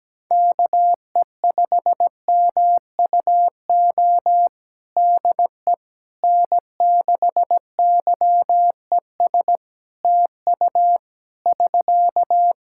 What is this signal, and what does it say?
Morse code, heard as a keyed tone, about 17 wpm, 700 Hz. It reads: KE5MUO DE N6YES TU <SK>
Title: K1USN SST QSO Practice 17wpm